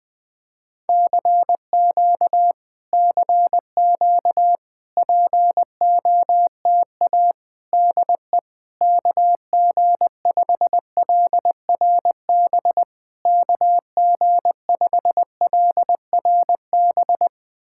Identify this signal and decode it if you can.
Morse code, heard as a keyed tone, about 21 wpm, 700 Hz. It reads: CQ CQ POTA DE KG5LRB KG5LRB
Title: POTA QSO Practice - 20wpm